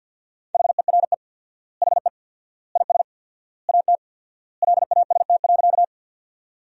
Morse code, heard as a keyed tone, about 52 wpm, 700 Hz. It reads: HERE HE IS AT LAST.
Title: 4x Word-Spacing - Sentences from Top 300 Words - 50wpm